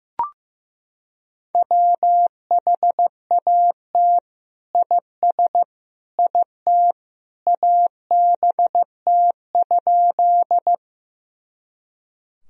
Morse code, heard as a keyed tone, about 16 wpm, 700 Hz. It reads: WHAT IS IT ABT?
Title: Sentences from Top 100 Words and Abbreviations 15wpm